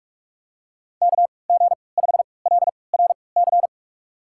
Morse code, heard as a keyed tone, about 46 wpm, spaced as for 25 wpm, 700 Hz. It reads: KG5LRC
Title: USA Call Signs - 6 Characters - 3x Character Spacing - 45wpm